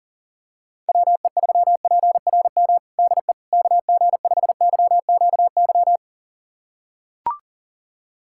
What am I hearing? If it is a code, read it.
Morse code, heard as a keyed tone, about 40 wpm, 700 Hz. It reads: WE3PRM DE KG5YQY
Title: USA Callsign DE Callsign - 6 Characters - 40wpm